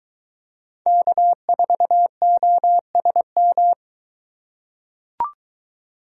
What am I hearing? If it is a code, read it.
Morse code, heard as a keyed tone, about 23 wpm, 700 Hz. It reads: K4OSM